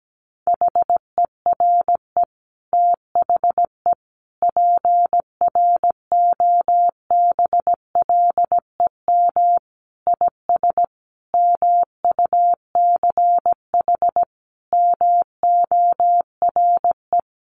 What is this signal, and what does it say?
Morse code, heard as a keyed tone, about 17 wpm, 700 Hz. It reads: HERE THE PROBLEM IS MUCH MORE
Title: Sentences from Top 1200 Words 17wpm